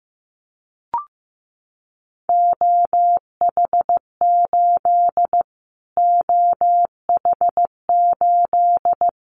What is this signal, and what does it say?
Morse code, heard as a keyed tone, about 15 wpm, 700 Hz. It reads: OH8 OH8
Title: Callsign Prefixes for Top 25 DXCC - Repeated 3x - 15wpm